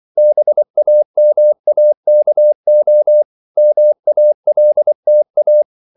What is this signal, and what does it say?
Morse code, heard as a keyed tone, about 24 wpm, 600 Hz. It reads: BAMAKO MALTA
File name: CW / Practice / NationalCapitals24